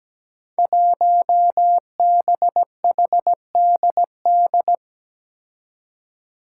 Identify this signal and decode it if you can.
Morse code, heard as a keyed tone, about 17 wpm, 700 Hz. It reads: 1BHDD
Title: Five Challenging Letter-Number Combinations 17wpm